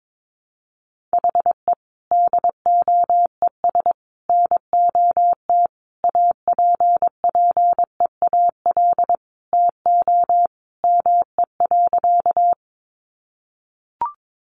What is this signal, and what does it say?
Morse code, heard as a keyed tone, about 22 wpm, 700 Hz. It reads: HE DOES NOT APPEAL TO ME.
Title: Sentences from Top 1300 Words 22wpm